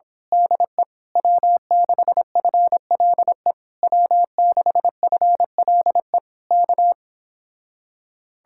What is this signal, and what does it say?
Morse code, heard as a keyed tone, about 26 wpm, 700 Hz. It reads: DE W6FLE W6FLE K
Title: Speed-Racing Common QSO Phrases - 22wpm